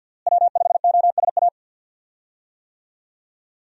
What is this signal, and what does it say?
Morse code, heard as a keyed tone, about 51 wpm, 700 Hz. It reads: W5OSA
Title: USA Call Signs - 5 Characters - Repeated 2x - 50wpm